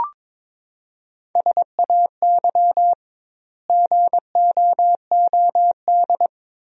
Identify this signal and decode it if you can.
Morse code, heard as a keyed tone, about 22 wpm, 700 Hz. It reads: SAY GOOD
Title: 2x Word-Spacing - Sentences from Top 300 Words - 22wpm